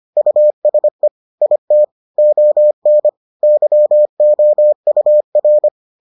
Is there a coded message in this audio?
Yes: Morse code from 0.2 to 5.7 s, heard as a keyed tone, about 25 wpm, 600 Hz. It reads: USE IT ON YOUR